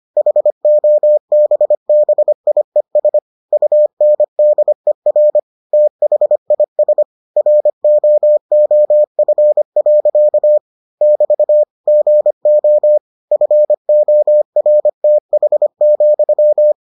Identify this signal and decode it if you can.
Morse code, heard as a keyed tone, about 25 wpm, 600 Hz. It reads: HOBBIES UNDER THIS ROOF. <BT> GO FORTH,